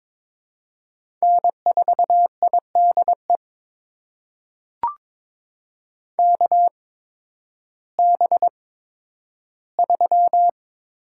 Morse code, heard as a keyed tone, about 22 wpm, spaced as for 12 wpm, 700 Hz. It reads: N4IDE KB3
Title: USA Call Signs - 5 Characters - 8x Character Spacing - 22wpm